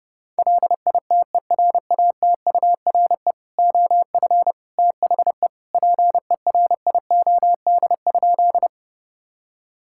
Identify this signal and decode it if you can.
Morse code, heard as a keyed tone, about 32 wpm, 700 Hz. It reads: LITERATURE OF THE PERIOD?
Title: Sentences from Top 1000 Words - Review Part 5 - 30wpm